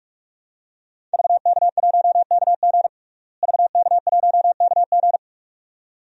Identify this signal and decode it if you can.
Morse code, heard as a keyed tone, about 41 wpm, 700 Hz. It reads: VK1KG VK1KG
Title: DXCC-100 Call Signs - 5 Characters - Repeated 2x - 45wpm